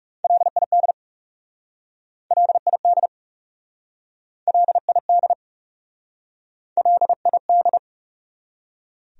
Morse code, heard as a keyed tone, about 41 wpm, 700 Hz. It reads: LID LID LID LID